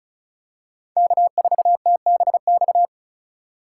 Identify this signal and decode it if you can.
Morse code, heard as a keyed tone, about 36 wpm, 700 Hz. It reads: K4TBX
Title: USA Call Signs - 5 Characters - 4x Character Spacing - 35wpm